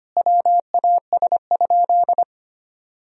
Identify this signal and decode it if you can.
Morse code, heard as a keyed tone, about 25 wpm, 700 Hz. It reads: WAS?
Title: Sentences from Top 1000 Words - Review Part 3 - 25wpm